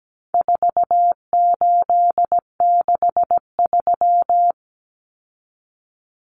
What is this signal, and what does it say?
Morse code, heard as a keyed tone, about 17 wpm, 700 Hz. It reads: 4863
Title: Common Cut Numbers - 4-Digit 17wpm